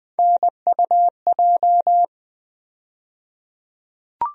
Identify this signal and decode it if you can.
Morse code, heard as a keyed tone, about 20 wpm, 700 Hz. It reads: NUJ
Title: USA Call Sign Postfixes - 3 Letters 20wpm